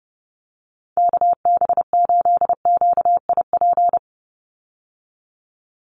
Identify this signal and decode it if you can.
Morse code, heard as a keyed tone, about 30 wpm, 700 Hz. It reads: KB8QIP
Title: USA Call Signs 30wpm